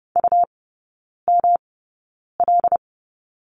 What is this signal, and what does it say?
Morse code, heard as a keyed tone, about 30 wpm, spaced as for 9 wpm, 700 Hz. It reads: UML
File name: Lesson 19e - Letter w - Three Characters 30wpm